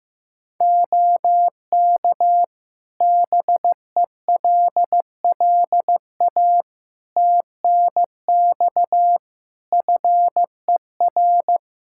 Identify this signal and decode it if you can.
Morse code, heard as a keyed tone, about 15 wpm, 700 Hz. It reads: OK BELLA TNX FER
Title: K1USN SST QSO Practice 15wpm